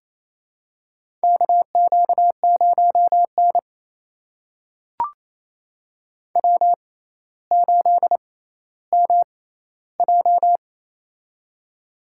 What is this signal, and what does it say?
Morse code, heard as a keyed tone, about 28 wpm, spaced as for 14 wpm, 700 Hz. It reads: KQ0N W8MJ
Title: USA Call Signs - 4 Characters - 6x Character Spacing - 28wpm